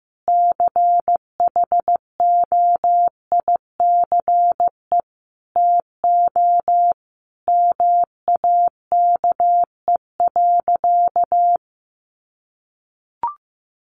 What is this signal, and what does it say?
Morse code, heard as a keyed tone, about 15 wpm, 700 Hz. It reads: CHOICE TO MAKE.
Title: Sentences from Top 900 Words 15wpm